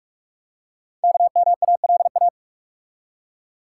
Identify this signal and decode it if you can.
Morse code, heard as a keyed tone, about 45 wpm, 700 Hz. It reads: KMALA
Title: Code Groups - 5 Letters - 45wpm